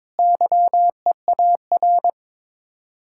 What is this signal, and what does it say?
Morse code, heard as a keyed tone, about 22 wpm, 700 Hz. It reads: YEAR
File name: practice / 22wpm / Sets of 7 Words - 22wpm